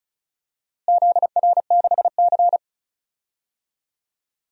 Morse code, heard as a keyed tone, about 34 wpm, 700 Hz. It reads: ZR6C
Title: DXCC-100 Call Signs - 4 Characters - Repeated 2x - 35wpm